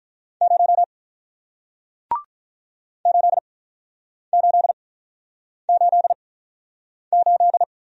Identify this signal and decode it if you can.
Morse code, heard as a keyed tone, about 48 wpm, spaced as for 9 wpm, 700 Hz. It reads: 0 8888